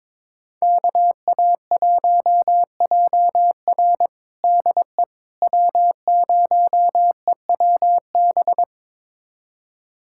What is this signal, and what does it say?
Morse code, heard as a keyed tone, about 22 wpm, 700 Hz. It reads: KA1JR DE W0EWB
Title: USA Callsign DE Callsign - 5 Characters - 22wpm